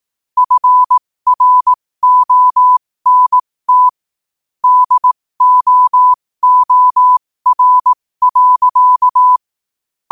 Morse code, heard as a keyed tone, about 18 wpm, spaced as for 15 wpm, 1000 Hz. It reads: FRONT DOOR.
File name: qotdfiles / 15wpm